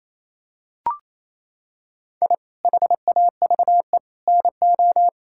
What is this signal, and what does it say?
Morse code, heard as a keyed tone, about 28 wpm, 700 Hz. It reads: I HAVE NO
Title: Sentences from Top 1000 Words - Review Part 6 - 28wpm